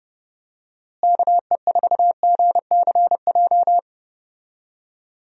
Morse code, heard as a keyed tone, about 31 wpm, 700 Hz. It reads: KE4GCJ
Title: USA Call Signs - 6 Characters - Repeated 3x - 30wpm